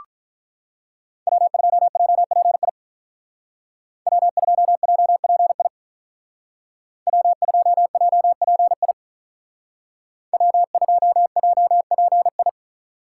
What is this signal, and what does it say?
Morse code, heard as a keyed tone, about 44 wpm, 700 Hz. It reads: W2JPI W2JPI W2JPI W2JPI
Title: Speed Racing - USA 1x3 Call Signs 35wpm